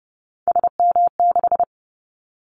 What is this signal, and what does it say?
Morse code, heard as a keyed tone, about 30 wpm, 700 Hz. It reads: SM6
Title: Callsign Prefixes for Top 25 DXCC - 30wpm